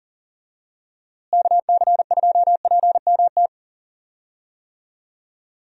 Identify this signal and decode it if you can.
Morse code, heard as a keyed tone, about 40 wpm, 700 Hz. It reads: KC2PMT
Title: USA Call Signs - 6 Characters - Repeated 2x - 40wpm